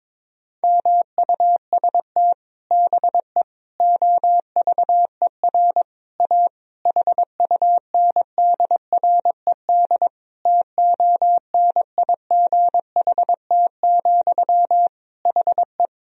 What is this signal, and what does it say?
Morse code, heard as a keyed tone, about 22 wpm, 700 Hz. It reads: MUST BE OVER A HUNDRED TONIGHT, HE
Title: Sentences from Top 1000 Words - Review Part 2 - 22wpm